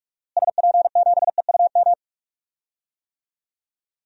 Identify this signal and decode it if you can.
Morse code, heard as a keyed tone, about 47 wpm, 700 Hz. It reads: SP7EUM
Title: DXCC-100 Call Signs - 6 Characters - Repeated 2x - 45wpm